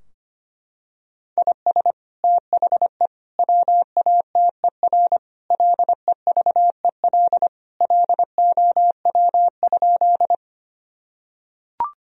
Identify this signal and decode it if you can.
Morse code, heard as a keyed tone, about 25 wpm, 700 Hz. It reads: IS THE WATER LEVEL LOW?
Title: Sentences from Top 1000 Words - Review Part 4 - 25wpm